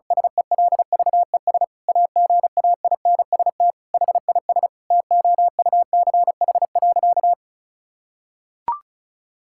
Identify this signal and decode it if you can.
Morse code, heard as a keyed tone, about 36 wpm, 700 Hz. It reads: SELVES AGAINST HIS TOUCH.
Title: Sentences from Top 1600 Words 35wpm